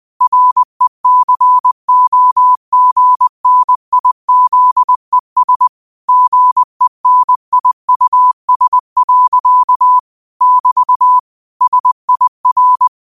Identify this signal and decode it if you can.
Morse code, heard as a keyed tone, about 20 wpm, 1000 Hz. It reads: RECOGNIZES GENIUS. <BT> SIR
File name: qotdfiles / 20wpm